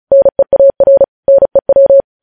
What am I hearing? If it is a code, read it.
Morse code, heard as a keyed tone, about 36 wpm, 550 Hz. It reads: NEAR NEW
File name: podcast / News headlines-1x-35x35